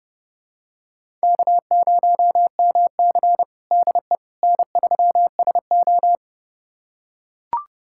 Morse code, heard as a keyed tone, about 31 wpm, 700 Hz. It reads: K0MC DE N3SO
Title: USA Callsign DE Callsign - 4 Characters - 30wpm